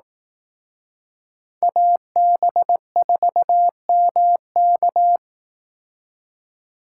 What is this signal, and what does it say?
Morse code, heard as a keyed tone, about 18 wpm, 700 Hz. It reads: AB4MK
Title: Speed Racing - USA Call Signs 15wpm